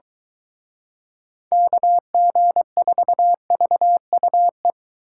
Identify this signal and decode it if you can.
Morse code, heard as a keyed tone, about 23 wpm, 700 Hz. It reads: KG4VUE